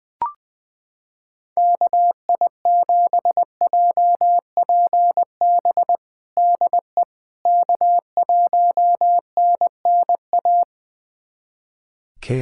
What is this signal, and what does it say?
Morse code, heard as a keyed tone, about 21 wpm, 700 Hz. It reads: KI7JPB DE K1NNA
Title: USA Callsign DE Callsign 20wpm